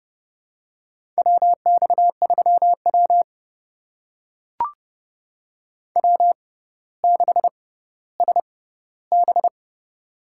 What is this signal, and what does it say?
Morse code, heard as a keyed tone, about 30 wpm, spaced as for 15 wpm, 700 Hz. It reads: WX3W W6SB
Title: USA Call Signs - 4 Characters - 6x Character Spacing - 30wpm